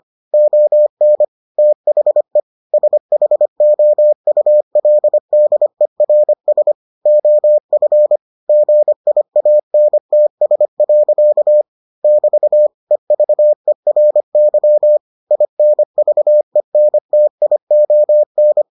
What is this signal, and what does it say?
Morse code, heard as a keyed tone, about 25 wpm, 600 Hz. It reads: ON THE SHOULDERS OF GIANTS. <BT> EVERY INVENTION